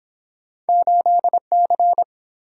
Morse code, heard as a keyed tone, about 29 wpm, 700 Hz. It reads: 8C